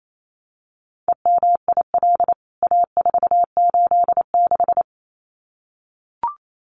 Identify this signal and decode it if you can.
Morse code, heard as a keyed tone, about 28 wpm, 700 Hz. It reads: EMIL A486
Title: CWT Exchange 28wpm